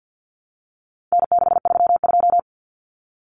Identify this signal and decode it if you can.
Morse code, heard as a keyed tone, about 50 wpm, 700 Hz. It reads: N632
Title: Common Cut Numbers - 4-Digit 50wpm